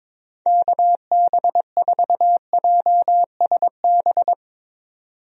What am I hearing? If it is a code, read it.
Morse code, heard as a keyed tone, about 22 wpm, 700 Hz. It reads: KB4JSB
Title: USA Call Signs - 6 Characters - 22wpm